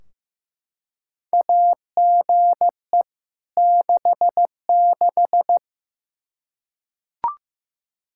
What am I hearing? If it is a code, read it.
Morse code, heard as a keyed tone, about 15 wpm, 700 Hz. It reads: AGE 66
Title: Practice QSO - Exchange 3 - More Info - 15wpm